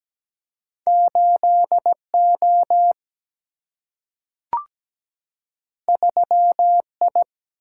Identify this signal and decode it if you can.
Morse code, heard as a keyed tone, about 17 wpm, 700 Hz. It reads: 8 O 3 I
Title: Single-Letter Single-Number 17wpm